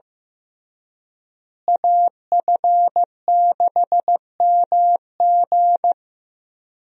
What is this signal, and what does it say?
Morse code, heard as a keyed tone, about 15 wpm, 700 Hz. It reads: AF6MG
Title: USA Call Signs - 5 Characters - Repeated 3x - 15wpm